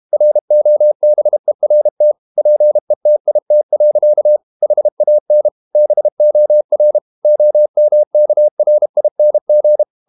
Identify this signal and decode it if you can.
Morse code, heard as a keyed tone, about 32 wpm, 600 Hz. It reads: ROBERT PETIT. HAN BOR OMKRING
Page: Ferie 32 wpm